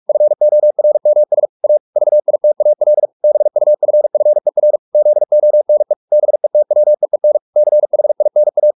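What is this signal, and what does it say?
Morse code, heard as a keyed tone, about 44 wpm, 600 Hz. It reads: FORMS A VITAL BUFFER ZONE BETWEEN CHINA
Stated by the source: Morse practice at 45 wpm, the novel Never by Ken Follett in English, from CW med Gnister.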